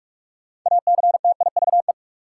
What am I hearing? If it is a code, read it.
Morse code, heard as a keyed tone, about 46 wpm, 700 Hz. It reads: ACTIVE
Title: Sentences from Top 900 Words - New Word Drill - 45wpm